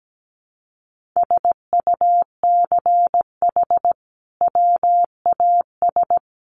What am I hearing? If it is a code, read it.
Morse code, heard as a keyed tone, about 17 wpm, 700 Hz. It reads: SUCH WAS
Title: Sentences from Top 1400 Words 17wpm